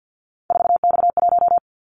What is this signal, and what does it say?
Morse code, heard as a keyed tone, about 50 wpm, 700 Hz. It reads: 4X1
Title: Callsign Prefixes for Top 50 DXCC - 50wpm